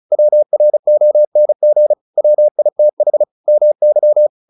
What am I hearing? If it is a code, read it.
Morse code, heard as a keyed tone, about 34 wpm, 600 Hz. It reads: WRONG WITH MY
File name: the old man and the sea 2 35wpm